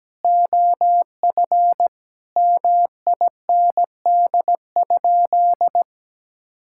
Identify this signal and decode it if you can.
Morse code, heard as a keyed tone, about 17 wpm, 700 Hz. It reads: OF MIND?
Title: Sentences from Top 1000 Words - Review Part 6 - 17wpm